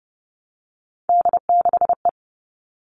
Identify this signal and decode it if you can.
Morse code, heard as a keyed tone, about 29 wpm, 700 Hz. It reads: D6E
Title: All Cut Numbers - 3-Digit 30wpm